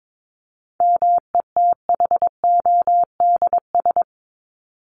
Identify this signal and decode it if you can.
Morse code, heard as a keyed tone, about 22 wpm, 700 Hz. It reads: METHODS
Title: ES:S Suffix - Top 100 Words with Base Words 22wpm